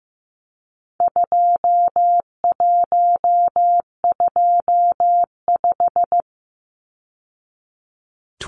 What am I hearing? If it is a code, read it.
Morse code, heard as a keyed tone, about 15 wpm, 700 Hz. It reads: 2125
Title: Numbers - 4-Digit 15wpm